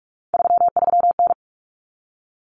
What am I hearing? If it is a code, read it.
Morse code, heard as a keyed tone, about 44 wpm, 700 Hz. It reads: 33N
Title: Common Cut Numbers - 3-Digit 45wpm